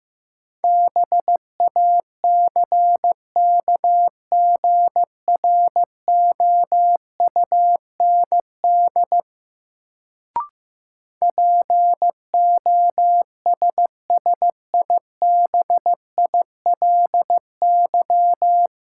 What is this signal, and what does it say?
Morse code, heard as a keyed tone, about 15 wpm, 700 Hz. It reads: BACKGROUND POSSIBILY
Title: Sentences from Top 1300 Words - New Word Drill Alt - 15wpm